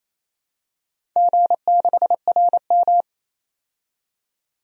Morse code, heard as a keyed tone, about 28 wpm, 700 Hz. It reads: G6RM
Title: DXCC-100 Call Signs - 4 Characters - 28wpm